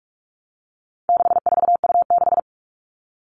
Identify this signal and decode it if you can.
Morse code, heard as a keyed tone, about 44 wpm, 700 Hz. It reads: 64U6
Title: All Cut Numbers - 4-Digit 45wpm